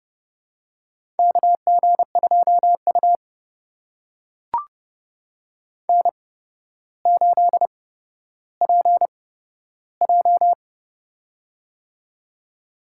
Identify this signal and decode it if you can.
Morse code, heard as a keyed tone, about 31 wpm, spaced as for 13 wpm, 700 Hz. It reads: KG2U N8PJ
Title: USA Call Signs - 4 Characters - 8x Character Spacing - 30wpm